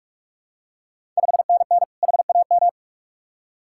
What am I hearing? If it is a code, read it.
Morse code, heard as a keyed tone, about 44 wpm, 700 Hz. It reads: 5NN HAM
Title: Ontario QSO Party Exchange 45wpm